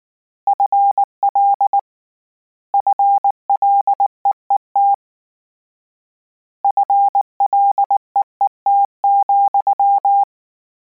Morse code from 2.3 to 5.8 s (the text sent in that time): FLEET